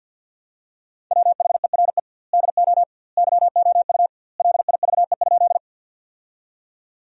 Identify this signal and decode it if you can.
Morse code, heard as a keyed tone, about 48 wpm, 700 Hz. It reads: WHERE DO YOU LIVE?
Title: 2x Word-Spacing - Sentences from Top 200 Words - 50wpm